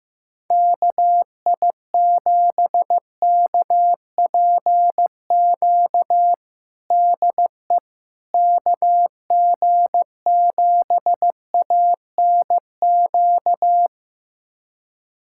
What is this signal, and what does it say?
Morse code, heard as a keyed tone, about 15 wpm, 700 Hz. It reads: KI7KPQ DE KG7ANQ
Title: USA Callsign DE Callsign - 6 Characters - 15wpm